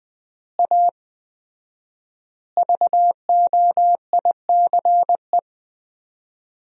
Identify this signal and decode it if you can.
Morse code, heard as a keyed tone, about 20 wpm, 700 Hz. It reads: A VOICE
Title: 4x Word-Spacing - Sentences from Top 500 Words - 20wpm